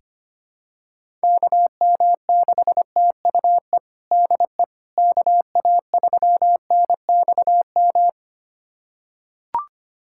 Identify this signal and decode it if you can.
Morse code, heard as a keyed tone, about 25 wpm, 700 Hz. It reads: KM6TUE DE KA3NXM
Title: USA Callsign DE Callsign 25wpm